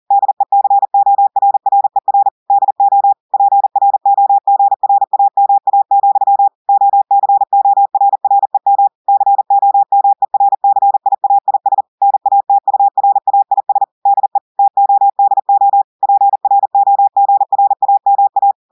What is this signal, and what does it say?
Morse code, heard as a keyed tone, about 40 wpm, 800 Hz. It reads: DECORRER DO PROGRAMA, OCORREM COMERCIAIS NATURAIS DE TODO PROGRAMA
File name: audios / portugais / portugais texte en clair 2 vitesse 40 mots minutes